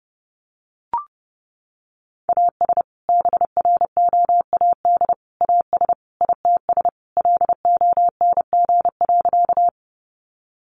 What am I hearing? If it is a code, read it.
Morse code, heard as a keyed tone, about 28 wpm, 700 Hz. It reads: AS BROAD AS ITS LONG.
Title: Sentences from Top 1000 Words 30wpm